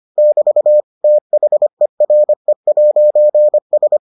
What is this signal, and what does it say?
Morse code, heard as a keyed tone, about 25 wpm, 600 Hz. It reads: <BT> THERE'S